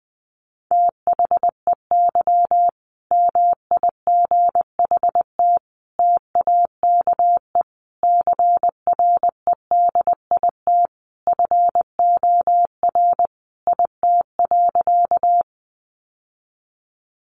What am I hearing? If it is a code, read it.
Morse code, heard as a keyed tone, about 20 wpm, 700 Hz. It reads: THEY MIGHT TAKE CREDIT FOR IT.
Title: Sentences from Top 1000 Words 20wpm